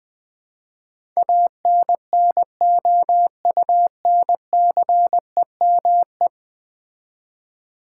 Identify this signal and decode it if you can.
Morse code, heard as a keyed tone, about 20 wpm, 700 Hz. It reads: ANNOUNCEME
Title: Build A Word - 5 Letters At A Time - 12-Letter Words - 20wpm